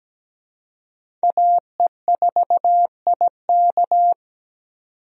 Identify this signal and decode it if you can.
Morse code, heard as a keyed tone, about 17 wpm, 700 Hz. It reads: AE4IK
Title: USA Call Signs - 5 Characters - 17wpm